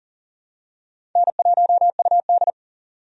Morse code, heard as a keyed tone, about 40 wpm, 700 Hz. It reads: N1UD
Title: USA Call Signs - 4 Characters - 40wpm